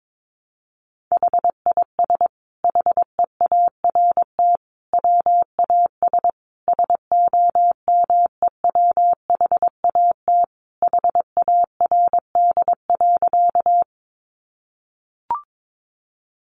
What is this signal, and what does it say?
Morse code, heard as a keyed tone, about 22 wpm, 700 Hz. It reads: HIS HEART WAS SOMEWHAT HARD.
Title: Sentences from Top 1300 Words 22wpm